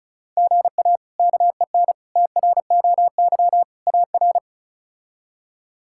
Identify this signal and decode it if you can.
Morse code, heard as a keyed tone, about 36 wpm, 700 Hz. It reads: GA KEN TROY AR
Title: K1USN SST QSO Practice - 35wpm